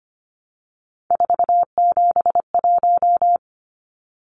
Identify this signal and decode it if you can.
Morse code, heard as a keyed tone, about 25 wpm, 700 Hz. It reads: 471
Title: Numbers - 3-Digit 25wpm